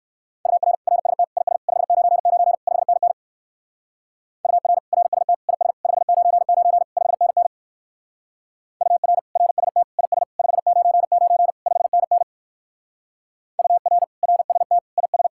Morse code, heard as a keyed tone, about 54 wpm, 700 Hz. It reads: UR RST IS 599 5NN UR RST IS 599 5NN UR RST IS 599 5NN UR RST IS
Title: Speed-Racing Common QSO Phrases - 45wpm